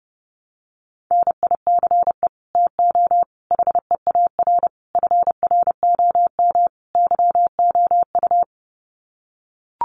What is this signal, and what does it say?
Morse code, heard as a keyed tone, about 30 wpm, 700 Hz. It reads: NICE TO HEAR FROM YOU
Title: Common QSO Phrases 30wpm